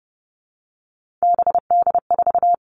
Morse code, heard as a keyed tone, about 30 wpm, 700 Hz. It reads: BD4
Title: All Cut Numbers - 3-Digit 30wpm